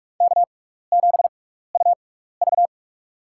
Morse code, heard as a keyed tone, about 47 wpm, spaced as for 16 wpm, 700 Hz. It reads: K7UV
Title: USA Call Signs - 4 Characters - 6x Character Spacing - 45wpm